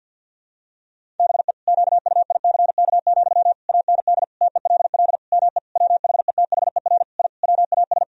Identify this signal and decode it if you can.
Morse code, heard as a keyed tone, about 50 wpm, 700 Hz. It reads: BE QUICK, AND TELL ME WHETHER I WAS
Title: Sentences from Top 1000 Words - Review Part 3 - 50wpm